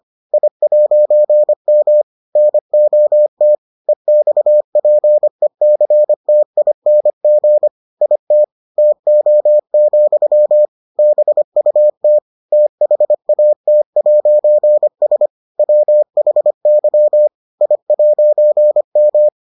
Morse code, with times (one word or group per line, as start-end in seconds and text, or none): 0.3-2.0 I'M
2.4-3.5 NOT
3.9-7.7 EXPECTING
8.0-8.4 IT
8.8-10.7 TO,
11.0-12.2 BUT
12.5-15.3 THAT'S
15.6-17.3 WHY
17.6-19.3 I'M